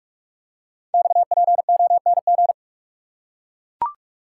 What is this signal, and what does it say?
Morse code, heard as a keyed tone, about 44 wpm, 690 Hz. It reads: XPONG
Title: Code Groups - 5 Letters - 45wpm